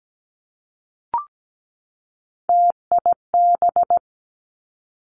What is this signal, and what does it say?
Morse code, heard as a keyed tone, about 17 wpm, 700 Hz. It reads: TIB
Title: Three Challenging Letter-Number Combinations 17wpm